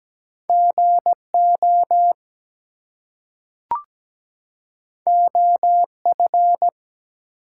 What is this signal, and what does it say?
Morse code, heard as a keyed tone, about 17 wpm, 700 Hz. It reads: GO OF
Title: Two Letter Words - 17wpm